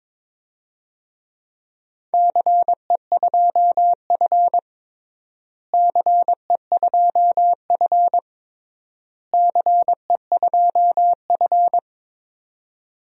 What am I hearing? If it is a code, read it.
Morse code, heard as a keyed tone, about 22 wpm, 700 Hz. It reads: CE2F CE2F CE2F
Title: DXCC-100 Call Signs - 4 Characters - Repeated 3x - 22wpm